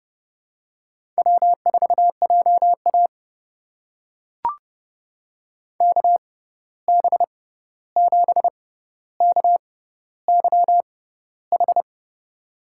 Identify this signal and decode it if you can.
Morse code, heard as a keyed tone, about 32 wpm, spaced as for 14 wpm, 700 Hz. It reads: W4JA KB7KYH
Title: USA Call Signs - 6x Character Spacing - 30wpm